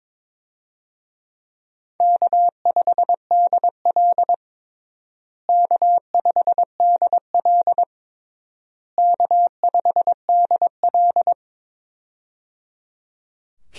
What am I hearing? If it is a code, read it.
Morse code, heard as a keyed tone, about 22 wpm, 700 Hz. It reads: K5DL K5DL K5DL
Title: USA Call Signs - 4 Characters - Repeated 3x - 22wpm